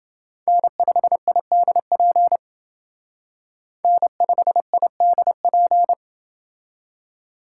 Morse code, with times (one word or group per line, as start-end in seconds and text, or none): 0.5-2.4 N5IDP
3.8-5.9 N5IDP